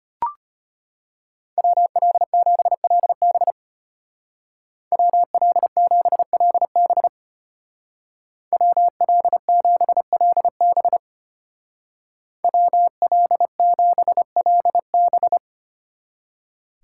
Morse code, heard as a keyed tone, about 33 wpm, 700 Hz. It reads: WL7LB WL7LB WL7LB WL7LB